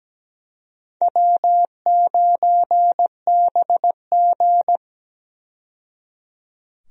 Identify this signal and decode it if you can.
Morse code, heard as a keyed tone, about 17 wpm, 700 Hz. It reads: W9BG
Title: USA Call Signs - 4 Characters - 17wpm